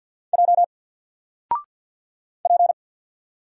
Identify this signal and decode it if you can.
Morse code, heard as a keyed tone, about 50 wpm, 700 Hz. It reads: J P
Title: Single Letter-Number Alt 50wpm